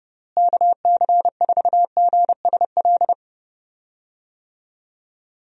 Morse code, heard as a keyed tone, about 32 wpm, 700 Hz. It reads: KC4GSL
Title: USA Call Signs - 6 Characters - Repeated 2x - 30wpm